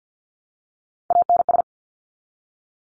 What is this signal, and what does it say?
Morse code, heard as a keyed tone, about 50 wpm, 700 Hz. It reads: ANS
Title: Top Three Letter Combinations in Words 50wpm